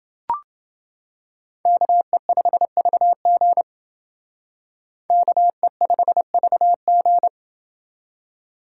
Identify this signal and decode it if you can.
Morse code, heard as a keyed tone, about 29 wpm, 700 Hz. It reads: KE5VG KE5VG